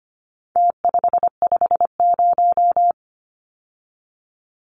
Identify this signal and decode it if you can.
Morse code, heard as a keyed tone, about 25 wpm, 700 Hz. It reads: T550
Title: Numbers - Contest Style - 4-Digit 25wpm